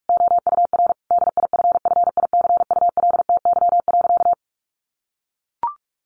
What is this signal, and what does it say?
Morse code, heard as a keyed tone, about 44 wpm, 700 Hz. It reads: OUR DIFFICULTY.
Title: Sentences from Top 1500 Words 45wpm